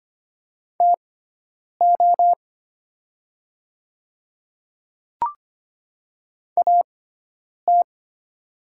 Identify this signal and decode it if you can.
Morse code, heard as a keyed tone, about 25 wpm, spaced as for 9 wpm, 700 Hz. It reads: TO AT
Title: 6x Character Spacing - Top 100 Words - 25wpm